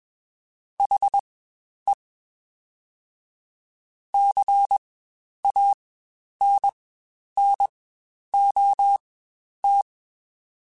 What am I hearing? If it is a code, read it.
Morse code, heard as a keyed tone, about 21 wpm, spaced as for 10 wpm, 800 Hz. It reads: HE CANNOT
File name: Spaced / Chuckles Spaced-mp3s / HalfFull sp @20